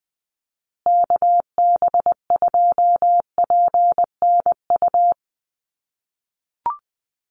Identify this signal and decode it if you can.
Morse code, heard as a keyed tone, about 19 wpm, 700 Hz. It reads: KB2PNU
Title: USA Call Signs Repeated 2x 20wpm